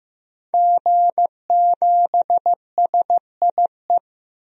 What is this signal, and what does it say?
Morse code, heard as a keyed tone, about 15 wpm, 700 Hz. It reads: G7SIE
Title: DXCC-100 Call Signs - 5 Characters - Repeated 2x - 15wpm